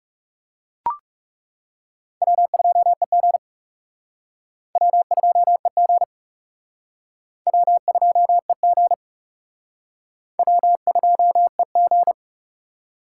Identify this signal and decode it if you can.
Morse code, heard as a keyed tone, about 37 wpm, 700 Hz. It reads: W2EG W2EG W2EG W2EG
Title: Speed Racing - USA Call Signs 30wpm